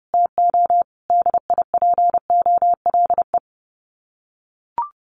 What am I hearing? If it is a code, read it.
Morse code, heard as a keyed tone, about 30 wpm, 700 Hz. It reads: TO DIPOLE
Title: Common QSO Phrases 30wpm